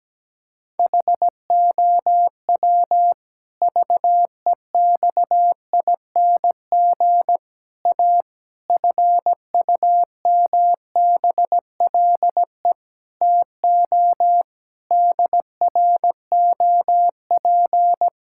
Morse code, heard as a keyed tone, about 17 wpm, 700 Hz. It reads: HOW VEXING A FUMBLE TO DROP
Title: Pangram Challenge v2 - 17wpm